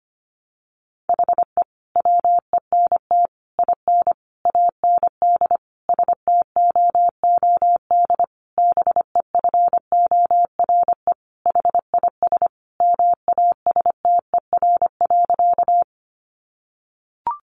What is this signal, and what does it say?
Morse code, heard as a keyed tone, about 25 wpm, 700 Hz. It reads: HE WENT IN AND STOOD BEFORE HIS MASTER.
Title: Sentences from Top 1300 Words 25wpm